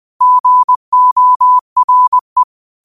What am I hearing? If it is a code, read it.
Morse code, heard as a keyed tone, about 20 wpm, 1000 Hz. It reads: GORE
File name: qotdfiles / 20wpm